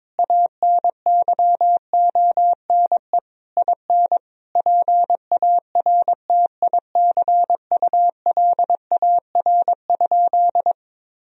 Morse code, heard as a keyed tone, about 22 wpm, 700 Hz. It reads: ANYONE IN PARTICULAR?
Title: Sentences from Top 1000 Words - Review Part 4 - 22wpm